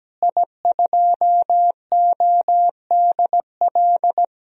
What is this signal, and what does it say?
Morse code, heard as a keyed tone, about 17 wpm, 700 Hz. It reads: I2ODL
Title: DXCC-100 Call Signs - 5 Characters - Repeated 3x - 17wpm